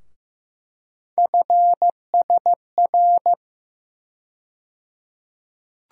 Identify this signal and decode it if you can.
Morse code, heard as a keyed tone, about 15 wpm, 700 Hz. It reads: FSR
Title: USA Call Sign Postfixes - 3 Letters 15wpm